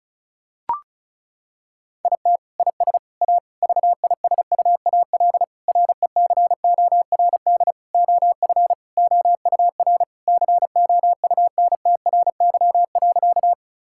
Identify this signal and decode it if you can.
Morse code, heard as a keyed tone, about 34 wpm, 700 Hz. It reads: IT IS A VISUAL RECORD OF OUR COUNTRY.
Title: Sentences from Top 1000 Words - Review Part 6 - 35wpm